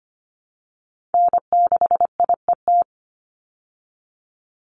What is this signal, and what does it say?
Morse code, heard as a keyed tone, about 25 wpm, 700 Hz. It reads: N6IET
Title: Call Signs Repeated 2x - 25wpm